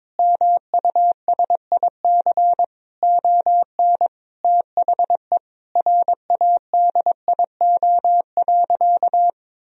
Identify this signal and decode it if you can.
Morse code, heard as a keyed tone, about 21 wpm, 700 Hz. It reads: MUSIC ON THE RADIO.
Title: Sentences from Top 1000 Words - Review Part 3 - 22wpm